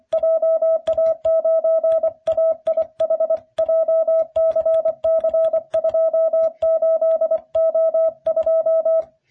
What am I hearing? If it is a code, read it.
Morse code, heard as a keyed tone, about 24 wpm, 660 Hz. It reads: JA9AIH JCC28O2